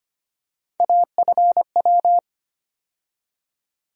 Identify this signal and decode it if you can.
Morse code, heard as a keyed tone, about 25 wpm, 700 Hz. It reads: AFW
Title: USA Call Sign Postfixes - 3 Letters 25wpm